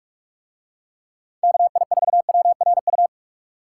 Morse code, heard as a keyed tone, about 46 wpm, 700 Hz. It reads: KI4WRU